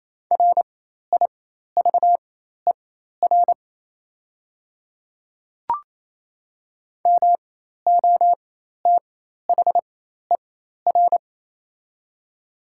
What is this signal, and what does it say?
Morse code, heard as a keyed tone, about 28 wpm, spaced as for 13 wpm, 700 Hz. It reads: RIVER MOTHER
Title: 4x Character Spacing - Top 300 Words - 28wpm